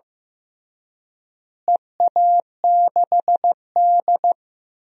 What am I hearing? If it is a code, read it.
Morse code, heard as a keyed tone, about 15 wpm, 700 Hz. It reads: EA6D
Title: DXCC-100 Call Signs - 4 Characters - Repeated 3x - 15wpm